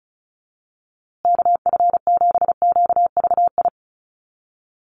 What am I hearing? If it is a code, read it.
Morse code, heard as a keyed tone, about 34 wpm, 700 Hz. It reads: KF7QVI
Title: USA Call Signs Repeated 3x 35wpm